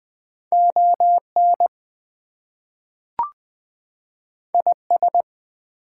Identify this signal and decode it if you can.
Morse code, heard as a keyed tone, about 20 wpm, 700 Hz. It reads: ON IS
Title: Two Letter Words - 20wpm